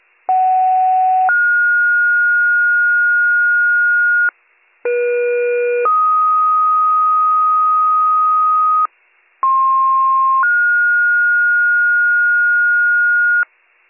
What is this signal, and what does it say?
Two-tone page, heard as a one-second tone, then a three-second tone, about 750 then 1430 Hz, about 480 then 1250 Hz, about 1060 then 1430 Hz, with noise. It is heard from 0.3 to 13.4 s.